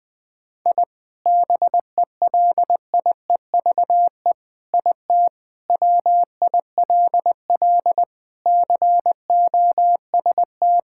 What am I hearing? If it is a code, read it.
Morse code, heard as a keyed tone, about 20 wpm, 700 Hz. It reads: I BELIEVE IT WILL COST
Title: Sentences from Top 1000 Words - Review Part 5 - 20wpm